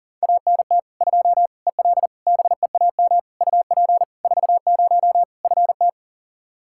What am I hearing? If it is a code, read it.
Morse code, heard as a keyed tone, about 37 wpm, 700 Hz. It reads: ANT 2 EL BEAM UP 40 FT
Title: Practice QSO - Exchange 2 - Info - 40wpm